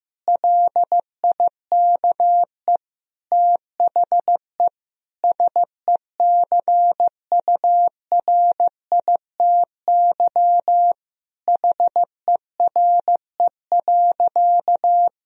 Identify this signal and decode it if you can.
Morse code, heard as a keyed tone, about 16 wpm, 700 Hz. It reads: LIKE THE SECURITY HERE.
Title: Sentences from Top 1000 Words - Review Part 4 - 15wpm